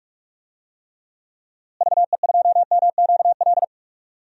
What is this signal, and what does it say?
Morse code, heard as a keyed tone, about 45 wpm, 700 Hz. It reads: VE2MQL
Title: DXCC-100 Call Signs - 6 Characters - Repeated 3x - 45wpm